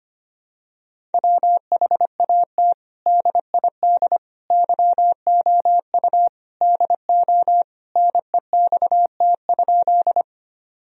Morse code, heard as a keyed tone, about 25 wpm, 700 Hz. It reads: WHAT DID YOU DO NEXT?
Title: Sentences from Top 1000 Words - Review Part 1 - 25wpm